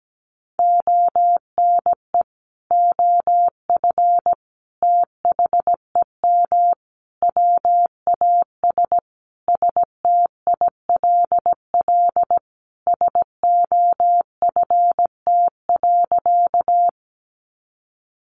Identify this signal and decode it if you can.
Morse code, heard as a keyed tone, about 17 wpm, 700 Hz. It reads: ONE OF THEM WAS STILL SOFT.
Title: Sentences from Top 1500 Words 17wpm